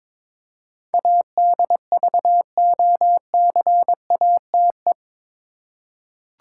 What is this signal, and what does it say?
Morse code, heard as a keyed tone, about 22 wpm, 700 Hz. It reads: ADVOCATE
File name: practice / 22wpm / Eight Letter Words - 22wpm